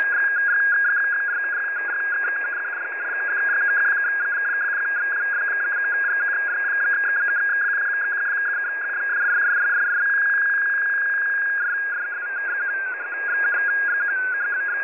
Two-tone teleprinter signal, about 200 Hz shift.
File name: file21042 / websdr recording start t15 39 15z 10710.5khz